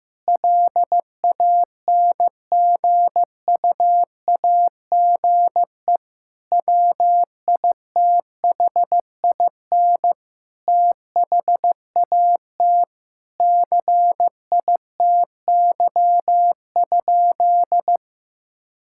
Morse code, heard as a keyed tone, about 15 wpm, 700 Hz. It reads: LANGUAGE WITHIN THAT CITY?
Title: Sentences from Top 1000 Words - Review Part 5 - 15wpm